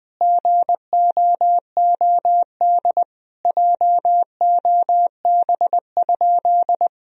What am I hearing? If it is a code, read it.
Morse code, heard as a keyed tone, about 20 wpm, 700 Hz. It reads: GOOD JOB?
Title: Sentences from Top 1000 Words - Review Part 3 - 20wpm